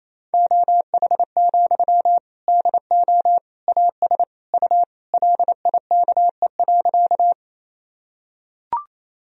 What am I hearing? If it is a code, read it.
Morse code, heard as a keyed tone, about 28 wpm, 700 Hz. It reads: OH, DO AS U LIKE.
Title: Sentences from Top 500 Words and Abbreviations 28wpm